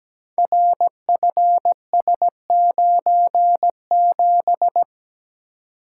Morse code, heard as a keyed tone, about 17 wpm, 700 Hz. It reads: RFS97
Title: Code Groups - 5 Letter:Numbers - 17wpm